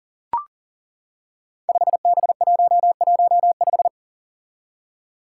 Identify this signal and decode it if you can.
Morse code, heard as a keyed tone, about 40 wpm, 700 Hz. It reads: 5B115
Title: Five Challenging Letter-Number Combinations 40wpm